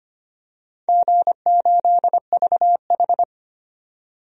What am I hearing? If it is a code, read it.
Morse code, heard as a keyed tone, about 25 wpm, 700 Hz. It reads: G8VH
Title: DXCC-100 Call Signs - 4 Characters - 25wpm